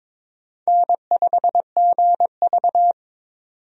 Morse code, heard as a keyed tone, about 22 wpm, 700 Hz. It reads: N5GV
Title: USA Call Signs - 4 Characters - 22wpm